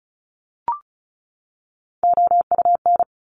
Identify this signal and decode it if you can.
Morse code, heard as a keyed tone, about 36 wpm, 700 Hz. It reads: OUN
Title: Top Three Letter Combinations in Words 35wpm